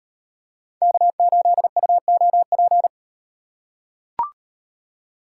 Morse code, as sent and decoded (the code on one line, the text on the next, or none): -.- ---.. ..- --- .--.
K8UOP